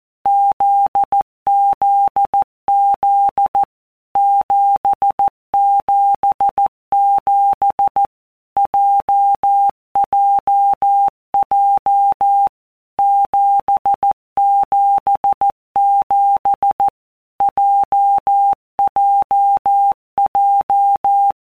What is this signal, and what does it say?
Morse code, heard as a keyed tone, about 14 wpm, 800 Hz. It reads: ZZZ 777 JJJ 777 JJJ
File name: Basic Letter Groups By WPM's / Groups@13 / JZ17@13